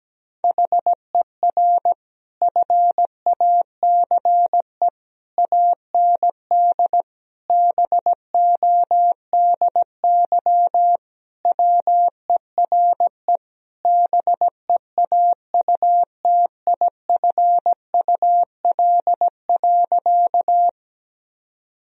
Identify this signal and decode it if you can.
Morse code, heard as a keyed tone, about 17 wpm, 700 Hz. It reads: HER FACE AND BODY WERE BEAUTIFUL.
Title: Sentences from Top 1000 Words - Review Part 2 - 17wpm